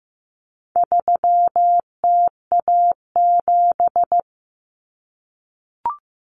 Morse code, heard as a keyed tone, about 15 wpm, 700 Hz. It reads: 3TA7
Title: Common Cut Numbers - 4-Digit 15wpm